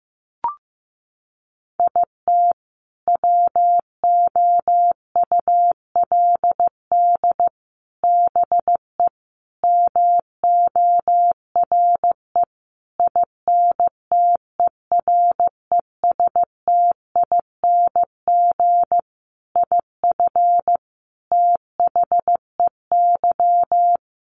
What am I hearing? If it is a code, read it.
Morse code, heard as a keyed tone, about 15 wpm, 700 Hz. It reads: IT WOULD BE MORE INTERESTING IF THEY
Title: Sentences from Top 1000 Words 15wpm